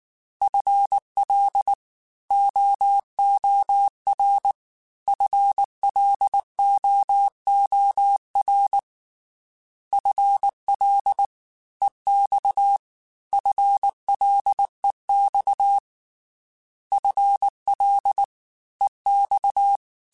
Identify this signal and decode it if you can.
Morse code, heard as a keyed tone, about 19 wpm, 800 Hz. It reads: FL OOR FLOOR FL EX FLEX FL EX
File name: FL / FL mp3s / Fast FL @18